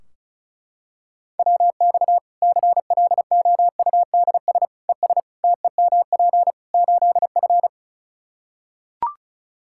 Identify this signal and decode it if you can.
Morse code, heard as a keyed tone, about 36 wpm, 700 Hz. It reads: WX CLOUDS ES TEMP 8F
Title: Practice QSO - Exchange 2 - Info - 35wpm